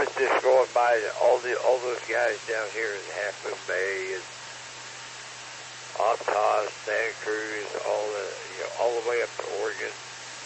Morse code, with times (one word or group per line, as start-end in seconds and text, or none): none